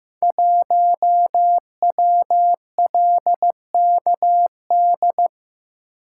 Morse code, heard as a keyed tone, about 15 wpm, 700 Hz. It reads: 1WLKD
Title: Code Groups - 5 Letter:Numbers - 15wpm